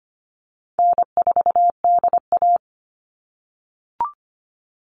Morse code, heard as a keyed tone, about 25 wpm, 700 Hz. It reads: N4DA
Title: NA SOTA Chaser Callsigns 25wpm